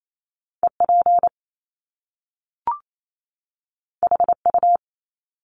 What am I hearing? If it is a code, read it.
Morse code, heard as a keyed tone, about 28 wpm, 700 Hz. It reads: EP HU
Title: Top Two Letter Combinations in Words 28wpm